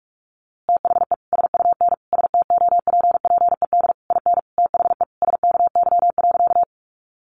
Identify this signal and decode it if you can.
Morse code, heard as a keyed tone, about 44 wpm, 700 Hz. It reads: THE SUN STOPPED IN THE SKY.
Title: A Sound Of Thunder 45wpm